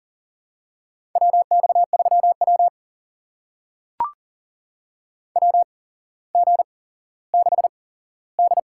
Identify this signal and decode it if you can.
Morse code, heard as a keyed tone, about 40 wpm, spaced as for 17 wpm, 700 Hz. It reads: WX3W WG6D
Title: USA Call Signs - 4 Characters - 8x Character Spacing - 40wpm